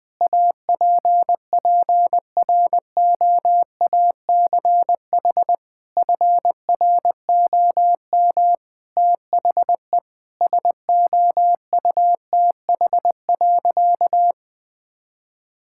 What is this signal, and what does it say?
Morse code, heard as a keyed tone, about 20 wpm, 700 Hz. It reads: APPROACH FROM THE SOUTH.
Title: Sentences from Top 1000 Words - Review Part 6 - 20wpm